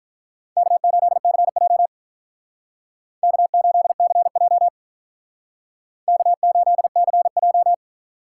Morse code, heard as a keyed tone, about 46 wpm, 700 Hz. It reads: K8CJ K8CJ K8CJ